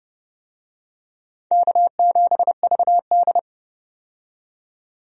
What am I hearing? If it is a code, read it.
Morse code, heard as a keyed tone, about 30 wpm, 700 Hz. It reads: K7VD
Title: USA Call Signs - 4 Characters - 30wpm